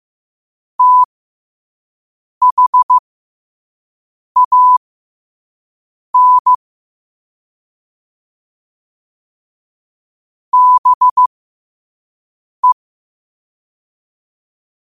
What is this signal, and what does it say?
Morse code, heard as a keyed tone, about 15 wpm, spaced as for 5 wpm, 1000 Hz. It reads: THAN BE
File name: qotdfiles / 5wpm